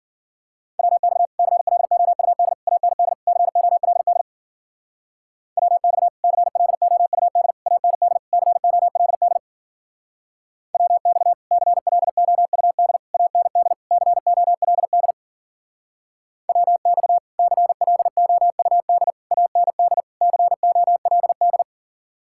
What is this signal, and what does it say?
Morse code, heard as a keyed tone, about 50 wpm, 700 Hz. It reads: WX CLOUD AND COLD WX CLOUD AND COLD WX CLOUD AND COLD WX CLOUD AND COLD
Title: Speed-Racing Common QSO Phrases - 40wpm